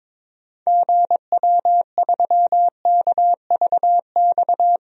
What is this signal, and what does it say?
Morse code, heard as a keyed tone, about 22 wpm, 700 Hz. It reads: GW3KVX
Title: DXCC-100 Call Signs - 6 Characters - 22wpm